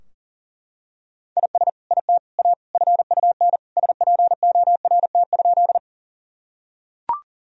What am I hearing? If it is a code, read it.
Morse code, heard as a keyed tone, about 40 wpm, 700 Hz. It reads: IS IT A FUN SPORT?
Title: Sentences from Top 1000 Words - Review Part 4 - 40wpm